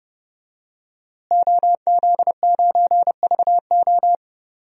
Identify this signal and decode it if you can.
Morse code, heard as a keyed tone, about 30 wpm, 700 Hz. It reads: OZ9VO
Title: DXCC-100 Call Signs - 5 Characters - Repeated 2x - 30wpm